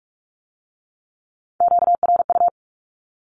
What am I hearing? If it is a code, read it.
Morse code, heard as a keyed tone, about 46 wpm, 700 Hz. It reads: QRU
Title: Q-Codes with Spoken Meaning - 45wpm